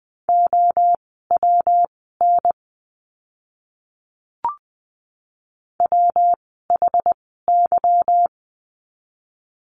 Morse code, heard as a keyed tone, about 20 wpm, spaced as for 15 wpm, 700 Hz. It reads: OWN WHY
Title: 146 Lesson 24f - Letter y - Words 20wpm